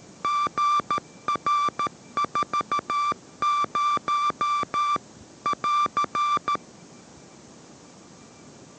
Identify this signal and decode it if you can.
Morse code, heard as a keyed tone, about 13 wpm, 1170 Hz. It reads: GR40 <AR>